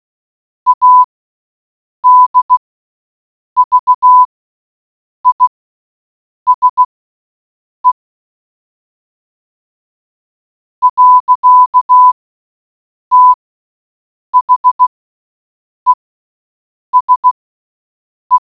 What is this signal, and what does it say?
Morse code, heard as a keyed tone, about 16 wpm, spaced as for 6 wpm, 1000 Hz. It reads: ADVISE. THESE